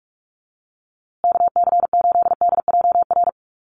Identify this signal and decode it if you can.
Morse code, heard as a keyed tone, about 44 wpm, 700 Hz. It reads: KC8DJR
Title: USA Call Signs Repeated 2x 45wpm